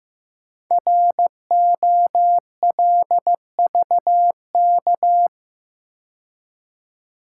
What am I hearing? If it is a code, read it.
Morse code, heard as a keyed tone, about 15 wpm, 700 Hz. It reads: ROLVK
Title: Code Groups - 5 Letter:Numbers - 15wpm